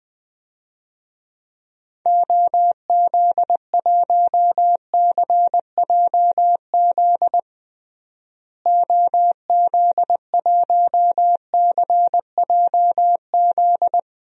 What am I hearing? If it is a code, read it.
Morse code, heard as a keyed tone, about 21 wpm, 700 Hz. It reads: OZ1CJZ OZ1CJZ
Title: DXCC-100 Call Signs - 6 Characters - Repeated 3x - 20wpm